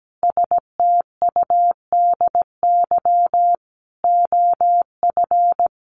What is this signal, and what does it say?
Morse code, heard as a keyed tone, about 17 wpm, 700 Hz. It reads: STUDY OF
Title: Sentences from Top 900 Words 17wpm